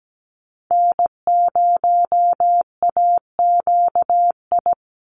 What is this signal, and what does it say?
Morse code, heard as a keyed tone, about 17 wpm, 700 Hz. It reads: N0AQI
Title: USA Call Signs 17wpm